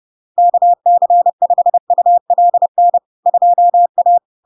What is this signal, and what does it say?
Morse code, heard as a keyed tone, about 30 wpm, 700 Hz. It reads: KC5ULN 2A
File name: LARK-FD-2022-Log-in-CW-30wpm-700-Hz